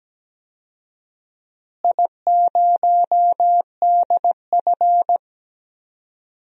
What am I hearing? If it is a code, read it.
Morse code, heard as a keyed tone, about 17 wpm, 700 Hz. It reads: I0DF
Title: DXCC-100 Call Signs - 4 Characters - Repeated 3x - 17wpm